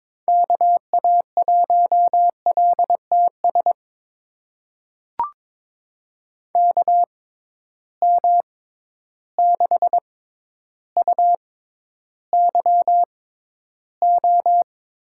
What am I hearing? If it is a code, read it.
Morse code, heard as a keyed tone, about 22 wpm, spaced as for 11 wpm, 700 Hz. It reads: KA1LTS KM6UYO
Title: USA Call Signs - 6 Characters - 6x Character Spacing - 22wpm